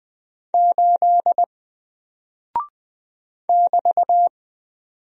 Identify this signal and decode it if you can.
Morse code, heard as a keyed tone, about 20 wpm, 700 Hz. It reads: 8 <BT>
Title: Single Character (Letter:Number:Punctuation:Prosign) Alt 20wpm